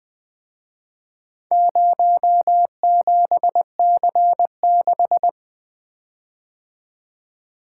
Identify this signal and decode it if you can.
Morse code, heard as a keyed tone, about 23 wpm, 700 Hz. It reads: 07C6
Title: Speed-Racing - Four Letter-Number Combinations - 20wpm